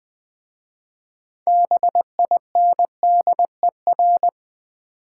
Morse code, heard as a keyed tone, about 20 wpm, 700 Hz. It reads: BINDER
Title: Easy Words with Hints - 20wpm